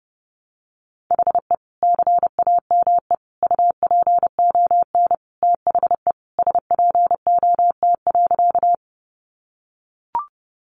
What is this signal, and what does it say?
Morse code, heard as a keyed tone, about 30 wpm, 700 Hz. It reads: HE CAME UPON THE SPOT.
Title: Sentences from Top 1100 Words 30wpm